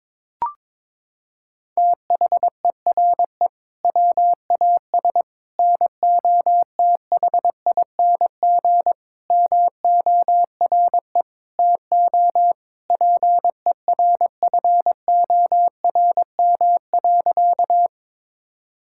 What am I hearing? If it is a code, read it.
Morse code, heard as a keyed tone, about 22 wpm, 700 Hz. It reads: THERE WAS NOTHING MORE TO PERFORM.
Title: Sentences from Top 1000 Words - Review Part 5 - 22wpm